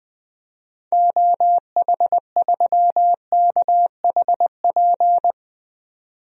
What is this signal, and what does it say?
Morse code, heard as a keyed tone, about 20 wpm, 700 Hz. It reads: OH3KHP
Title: DXCC-100 Call Signs - 6 Characters - 20wpm